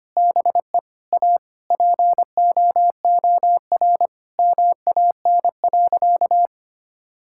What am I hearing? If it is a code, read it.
Morse code, heard as a keyed tone, about 25 wpm, 700 Hz. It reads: BE A POOR MAN.
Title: Sentences from Top 1000 Words - Review Part 5 - 25wpm